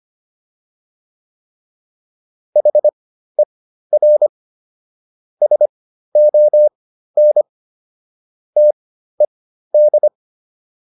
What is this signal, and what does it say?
Morse code, heard as a keyed tone, about 25 wpm, spaced as for 13 wpm, 600 Hz. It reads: HER SON TED